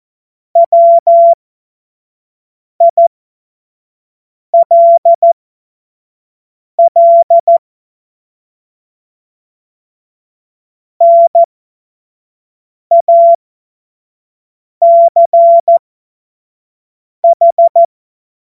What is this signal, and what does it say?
Morse code, heard as a keyed tone, about 14 wpm, spaced as for 5 wpm, 680 Hz. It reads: WILL NACH
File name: Morsekurs / Witze / Witz 07. Februar 2023 25 F70